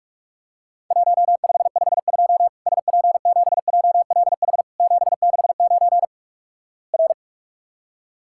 Morse code, heard as a keyed tone, about 46 wpm, 700 Hz. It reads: 1552 SP7JLH 769 R
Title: Worked All Europe DX Contest - QTC Exchange - 45wpm